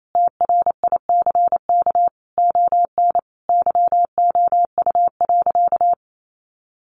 Morse code, heard as a keyed tone, about 28 wpm, 700 Hz. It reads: TRICK ON YOU.
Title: Sentences from Top 1500 Words 28wpm